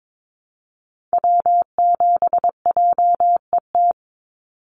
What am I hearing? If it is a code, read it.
Morse code, heard as a keyed tone, about 22 wpm, 700 Hz. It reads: W7JET
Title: NA SOTA Activator Callsigns 22wpm